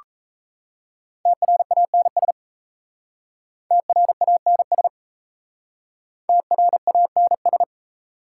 Morse code, heard as a keyed tone, about 35 wpm, 700 Hz. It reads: TRANS TRANS TRANS